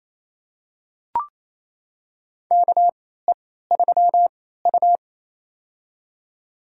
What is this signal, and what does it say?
Morse code, heard as a keyed tone, about 28 wpm, spaced as for 16 wpm, 700 Hz. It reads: KE3U
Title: USA Call Signs - 4 Characters - 3x Character Spacing - 28wpm